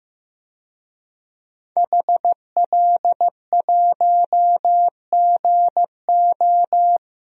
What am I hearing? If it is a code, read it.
Morse code, heard as a keyed tone, about 15 wpm, 700 Hz. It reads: HL1GO
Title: DXCC-100 Call Signs - 5 Characters - 15wpm